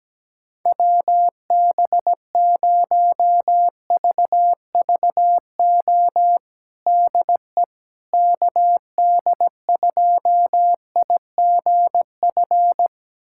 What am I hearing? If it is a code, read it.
Morse code, heard as a keyed tone, about 17 wpm, 700 Hz. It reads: WB0VVO DE KD2IGF
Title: USA Callsign DE Callsign - 6 Characters - 17wpm